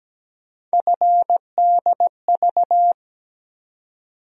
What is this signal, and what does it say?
Morse code, heard as a keyed tone, about 17 wpm, 700 Hz. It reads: FDV
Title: USA Call Sign Postfixes - 3 Letters 17wpm